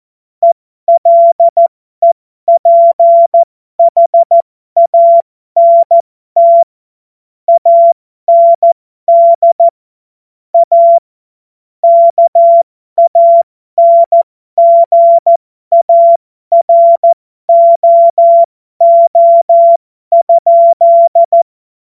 Morse code, heard as a keyed tone, about 14 wpm, spaced as for 12 wpm, 680 Hz. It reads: ELEPHANT AND A KANGAROO?
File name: Morsekurs / Witze